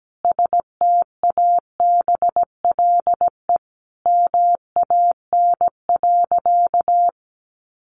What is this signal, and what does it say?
Morse code, heard as a keyed tone, about 17 wpm, 700 Hz. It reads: STABLE MAN.
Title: Sentences from Top 1500 Words 17wpm